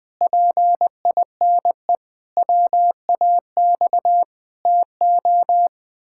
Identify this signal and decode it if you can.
Morse code, heard as a keyed tone, about 19 wpm, 700 Hz. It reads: PINE WAX TO
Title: Pangram Challenge v2 - 20wpm